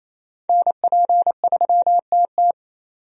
Morse code, heard as a keyed tone, about 28 wpm, 700 Hz. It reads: NP3TT
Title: USA Call Signs - 5 Characters - 28wpm